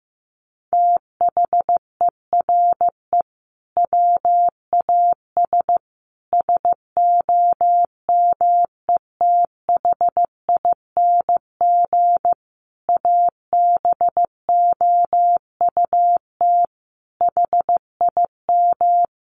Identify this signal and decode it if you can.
Morse code, heard as a keyed tone, about 14 wpm, 700 Hz. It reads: THERE WAS SOMETHING ABOUT HIM
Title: Sentences from Top 1500 Words 15wpm